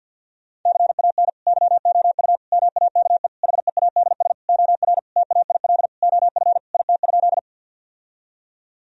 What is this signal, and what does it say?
Morse code, heard as a keyed tone, about 50 wpm, 700 Hz. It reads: CAN YOU MAKE HEADS OR TAIL OF IT?
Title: Sentences from Top 1000 Words - Review Part 3 - 50wpm